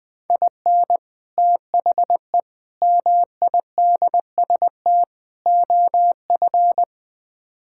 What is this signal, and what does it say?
Morse code, heard as a keyed tone, about 20 wpm, 700 Hz. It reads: IN THE MIDST OF
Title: Build A Phrase - 4-Word Phrases - 20wpm